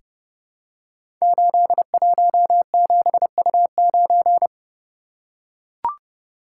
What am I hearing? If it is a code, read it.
Morse code, heard as a keyed tone, about 30 wpm, 700 Hz. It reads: 817U9
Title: Code Groups - 5 Letter:Numbers - 30wpm